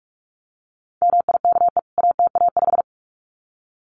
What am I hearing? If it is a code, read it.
Morse code, heard as a keyed tone, about 44 wpm, 700 Hz. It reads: MIKE ATA5
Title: CWT Exchange 45wpm